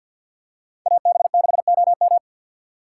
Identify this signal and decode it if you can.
Morse code, heard as a keyed tone, about 51 wpm, 690 Hz. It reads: AB6OM
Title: USA Call Signs - 5 Characters - 3x Character Spacing - 50wpm